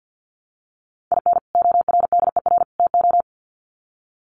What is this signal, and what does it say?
Morse code, heard as a keyed tone, about 49 wpm, 700 Hz. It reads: IN ORDER TO
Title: Sets of 3 Words 50wpm